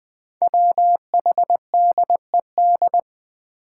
Morse code, heard as a keyed tone, about 20 wpm, 700 Hz. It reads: WHDED
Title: Five Challenging Letter-Number Combinations 20wpm